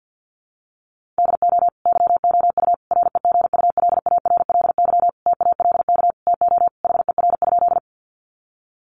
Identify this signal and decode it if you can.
Morse code, heard as a keyed tone, about 50 wpm, 700 Hz. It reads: DO YOU REGULARLY TALK TO HER?
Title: Sentences from Top 1400 Words 50wpm